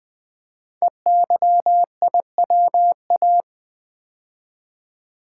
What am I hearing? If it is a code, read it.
Morse code, heard as a keyed tone, about 20 wpm, 700 Hz. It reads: EYIWA
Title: Code Groups - 5 Letters - 20wpm